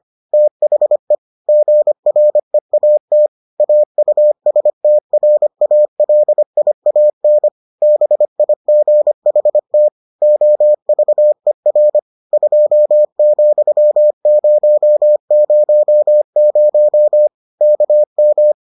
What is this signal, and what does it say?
Morse code, heard as a keyed tone, about 25 wpm, 600 Hz. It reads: THE GREAT AUSTRALIAN BIGHT OVER 2,000 KM